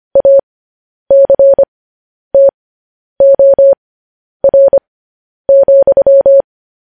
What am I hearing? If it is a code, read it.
Morse code, heard as a keyed tone, about 25 wpm, spaced as for 10 wpm, 550 Hz. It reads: ACTOR,
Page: news headlines in morse code at 10 wpm